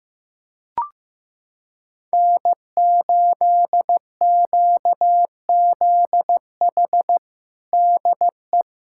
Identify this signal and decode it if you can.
Morse code, heard as a keyed tone, about 15 wpm, 700 Hz. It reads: N8QZH DE
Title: USA Callsign DE Callsign - 5 Characters - 15wpm